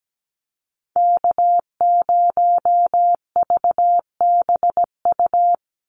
Morse code, heard as a keyed tone, about 17 wpm, 700 Hz. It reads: K0VBU
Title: Call Signs - 17wpm